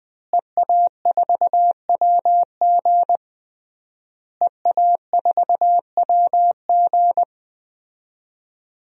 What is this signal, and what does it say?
Morse code, heard as a keyed tone, about 20 wpm, 700 Hz. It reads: EA4WG EA4WG
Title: DXCC-100 Call Signs - 5 Characters - Repeated 3x - 20wpm